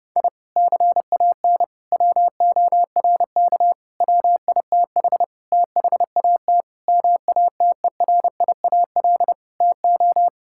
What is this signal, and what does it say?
Morse code, heard as a keyed tone, about 30 wpm, 700 Hz. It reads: I CAN WORK WITH THAT MATERIAL TO
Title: Sentences from Top 1000 Words - Review Part 5 - 30wpm